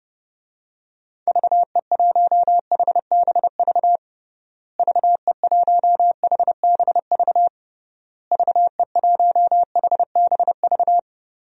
Morse code, heard as a keyed tone, about 30 wpm, 700 Hz. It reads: VE1HBV VE1HBV VE1HBV
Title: DXCC-100 Call Signs - 6 Characters - Repeated 3x - 30wpm